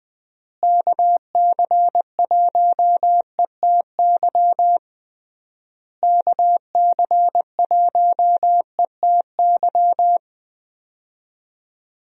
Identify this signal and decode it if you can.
Morse code, heard as a keyed tone, about 20 wpm, 700 Hz. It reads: KC1ETY KC1ETY
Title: USA Call Signs - 6 Characters - Repeated 3x - 20wpm